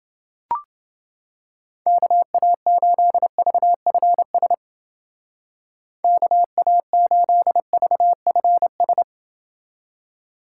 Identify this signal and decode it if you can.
Morse code, heard as a keyed tone, about 28 wpm, 700 Hz. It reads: KA8VFS KA8VFS